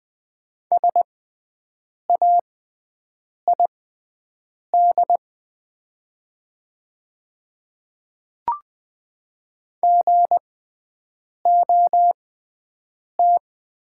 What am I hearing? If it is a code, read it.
Morse code, heard as a keyed tone, about 20 wpm, spaced as for 7 wpm, 700 Hz. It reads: SAID GOT
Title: 6x Character Spacing - Top 300 Words - 20wpm